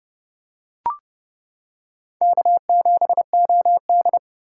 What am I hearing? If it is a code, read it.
Morse code, heard as a keyed tone, about 30 wpm, 700 Hz. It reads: K7OD